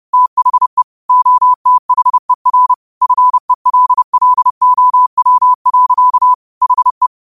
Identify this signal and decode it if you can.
Morse code, heard as a keyed tone, about 30 wpm, 1000 Hz. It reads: THE OTHER FELLOW. HE
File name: qotdfiles / 30wpm